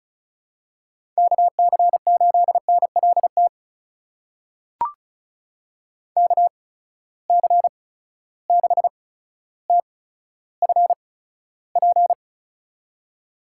Morse code, heard as a keyed tone, about 34 wpm, spaced as for 15 wpm, 700 Hz. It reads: KC8NLT KC6TFP
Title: USA Call Signs - 6 Characters - 8x Character Spacing - 35wpm